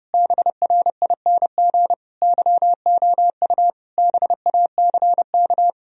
Morse code, heard as a keyed tone, about 30 wpm, 700 Hz. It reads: BRING YOU BACK
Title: Sentences from Top 1000 Words - Review Part 3 - 30wpm